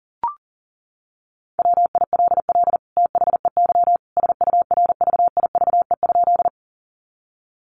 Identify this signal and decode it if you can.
Morse code, heard as a keyed tone, about 40 wpm, 700 Hz. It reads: WILL THEY SURVIVE?
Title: Sentences from Top 1300 Words 40wpm